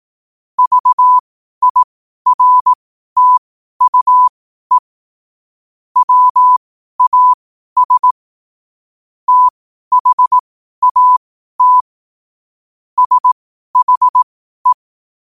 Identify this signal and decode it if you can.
Morse code, heard as a keyed tone, about 18 wpm, spaced as for 12 wpm, 1000 Hz. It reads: VIRTUE WAS THAT SHE
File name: qotdfiles / 12wpm